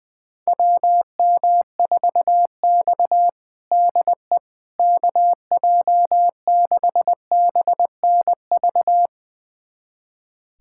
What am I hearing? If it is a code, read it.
Morse code, heard as a keyed tone, about 20 wpm, 700 Hz. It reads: WM4X DE KJ6BNV
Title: USA Callsign DE Callsign 20wpm